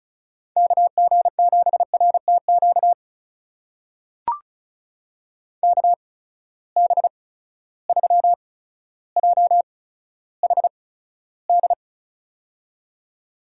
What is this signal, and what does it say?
Morse code, heard as a keyed tone, about 34 wpm, spaced as for 15 wpm, 700 Hz. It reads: KG7RTQ KB3JHD
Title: USA Call Signs - 6 Characters - 8x Character Spacing - 35wpm